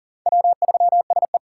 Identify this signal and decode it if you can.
Morse code, heard as a keyed tone, about 40 wpm, 700 Hz. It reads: W3SE
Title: USA Call Signs - 4 Characters - 4x Character Spacing - 40wpm